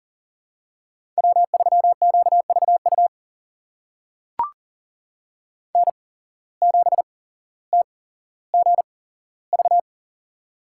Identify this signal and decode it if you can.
Morse code, heard as a keyed tone, about 39 wpm, spaced as for 17 wpm, 700 Hz. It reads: W3QVU N7TGV
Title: USA Call Signs - 5 Characters - 8x Character Spacing - 40wpm